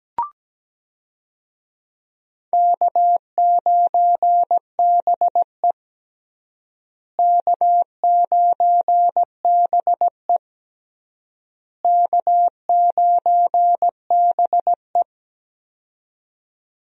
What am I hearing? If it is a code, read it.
Morse code, heard as a keyed tone, about 17 wpm, 700 Hz. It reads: K9BE K9BE K9BE
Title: USA Call Signs - 4 Characters - Repeated 3x - 17wpm